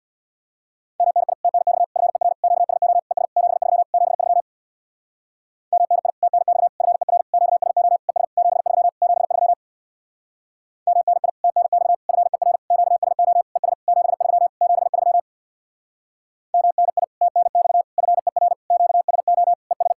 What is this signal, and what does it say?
Morse code, heard as a keyed tone, about 65 wpm, 700 Hz. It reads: MNI TNX FER QSO ES 73 73 MNI TNX FER QSO ES 73 73 MNI TNX FER QSO ES 73 73 MNI TNX FER QSO ES